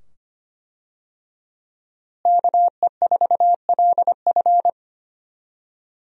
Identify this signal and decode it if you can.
Morse code, heard as a keyed tone, about 25 wpm, 700 Hz. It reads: KE4LF
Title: USA Call Signs - 5 Characters - Repeated 2x - 25wpm